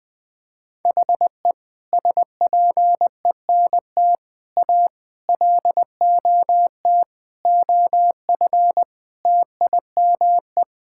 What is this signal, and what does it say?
Morse code, heard as a keyed tone, about 20 wpm, 700 Hz. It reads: HE SPENT A LOT OF TIME
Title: Sentences with 5-Word Phrases - 20wpm